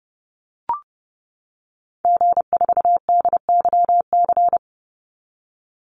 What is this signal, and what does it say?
Morse code, heard as a keyed tone, about 32 wpm, 700 Hz. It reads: G4DYC
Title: Call Signs - 30wpm